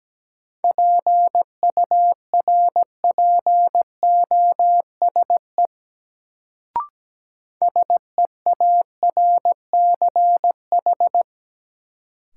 Morse code, heard as a keyed tone, about 17 wpm, 700 Hz. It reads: PURPOSE SEARCH
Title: Sentences from Top 800 Words - New Word Drill - 17wpm